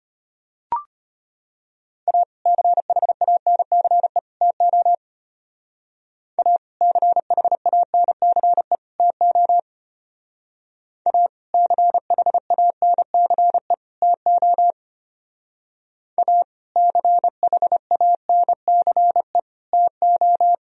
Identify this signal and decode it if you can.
Morse code, heard as a keyed tone, about 31 wpm, 700 Hz. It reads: A CHANCE TO A CHANCE TO A CHANCE TO A CHANCE TO
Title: Speed-Racing - Sets of 3 Words 25wpm